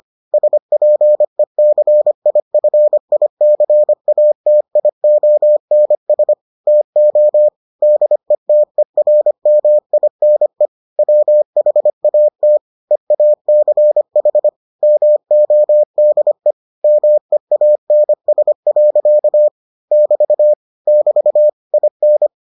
Morse code, heard as a keyed tone, about 25 wpm, 600 Hz. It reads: SPECIFICATIONS TO DETERMINE WHAT EACH MODE MEANS. <BT> <BT> IN